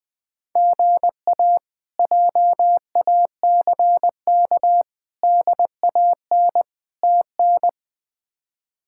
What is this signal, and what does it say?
Morse code, heard as a keyed tone, about 21 wpm, 700 Hz. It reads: GA JACK DAN TN
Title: K1USN SST QSO Practice - 20wpm